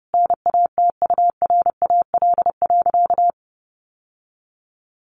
Morse code, heard as a keyed tone, about 30 wpm, 700 Hz. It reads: NATURAL.
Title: Sentences from Top 700 Words 30wpm